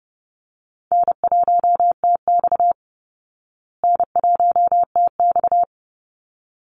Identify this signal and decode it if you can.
Morse code, heard as a keyed tone, about 30 wpm, 700 Hz. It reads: N1TX N1TX
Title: Call Signs Repeated 3x - 30wpm